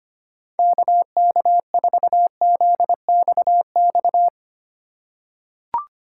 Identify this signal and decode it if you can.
Morse code, heard as a keyed tone, about 25 wpm, 700 Hz. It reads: KK4ZXX
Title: USA Call Signs - 6 Characters - 2x Character Spacing - 25wpm